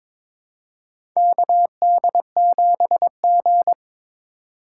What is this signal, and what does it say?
Morse code, heard as a keyed tone, about 22 wpm, 700 Hz. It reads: KD7G
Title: USA Call Signs - 4 Characters - 8x Character Spacing - 22wpm